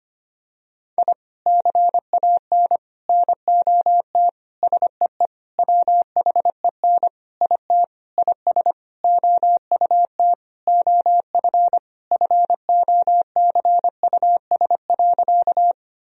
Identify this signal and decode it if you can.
Morse code, heard as a keyed tone, about 25 wpm, 700 Hz. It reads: I CAN NOT SEE WHEN IT IS OUT OF FOCUS.
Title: Sentences from Top 1000 Words - Review Part 4 - 25wpm